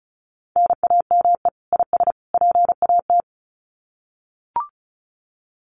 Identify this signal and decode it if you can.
Morse code, heard as a keyed tone, about 34 wpm, 700 Hz. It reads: NAME IS PAT
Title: Common QSO Phrases 35wpm